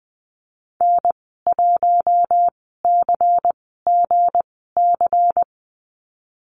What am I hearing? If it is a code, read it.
Morse code, heard as a keyed tone, about 20 wpm, spaced as for 14 wpm, 700 Hz. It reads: N1CGC
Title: 173 Lesson 28g - Number 9 - Call Signs 20wpm